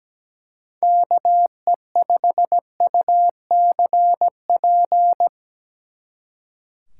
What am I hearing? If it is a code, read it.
Morse code, heard as a keyed tone, about 20 wpm, 700 Hz. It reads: KE5UCP
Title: Speed Racing - USA Call Signs 17wpm